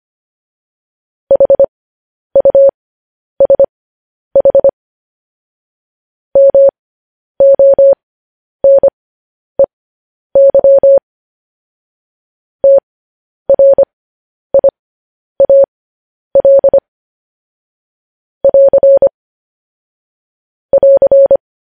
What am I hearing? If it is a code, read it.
Morse code, heard as a keyed tone, about 25 wpm, spaced as for 10 wpm, 550 Hz. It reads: HUSH MONEY TRIAL <AR> <AR>